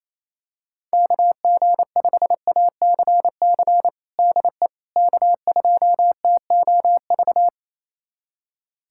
Morse code, heard as a keyed tone, about 28 wpm, 700 Hz. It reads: KG5ACC DE K2TOV
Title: USA Callsign DE Callsign 28wpm